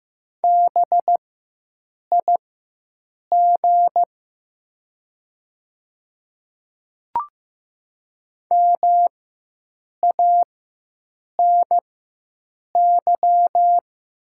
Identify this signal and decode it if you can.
Morse code, heard as a keyed tone, about 15 wpm, spaced as for 7 wpm, 700 Hz. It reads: BIG MANY
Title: 4x Character Spacing - Top 200 Words - 15wpm